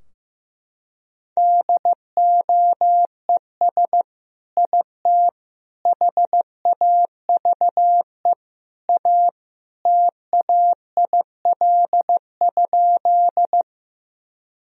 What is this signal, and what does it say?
Morse code, heard as a keyed tone, about 15 wpm, 700 Hz. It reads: DOES IT HAVE A TAIL?
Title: Sentences from Top 1000 Words - Review Part 3 - 15wpm